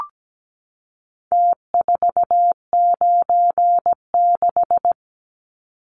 Morse code, heard as a keyed tone, about 17 wpm, 700 Hz. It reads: T496
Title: Numbers - Contest Style - 4-Digit 17wpm